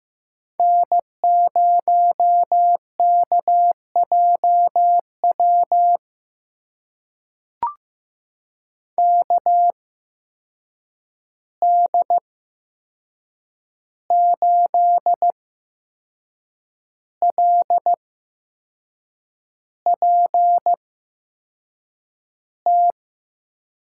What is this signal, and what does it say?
Morse code, heard as a keyed tone, about 15 wpm, spaced as for 6 wpm, 700 Hz. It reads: N0KJW KD8LPT
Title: USA Call Signs - 8x Character Spacing - 15wpm